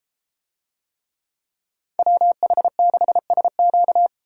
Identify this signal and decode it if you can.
Morse code, heard as a keyed tone, about 32 wpm, 700 Hz. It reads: WH6SQ